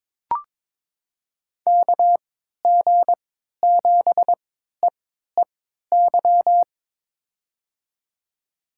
Morse code, heard as a keyed tone, about 22 wpm, spaced as for 13 wpm, 700 Hz. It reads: KG7EEY
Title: USA Call Signs - 3x Character Spacing - 22wpm